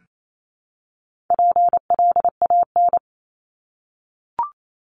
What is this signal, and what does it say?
Morse code, heard as a keyed tone, about 28 wpm, 700 Hz. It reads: PLAN
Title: ING Suffix - Top 100 Words with Base Words 28wpm